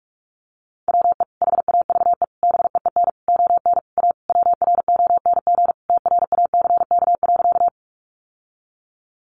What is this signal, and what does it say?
Morse code, heard as a keyed tone, about 44 wpm, 700 Hz. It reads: WE HAVE BEEN ON A WRONG TRACK.
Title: Sentences from Top 1100 Words 45wpm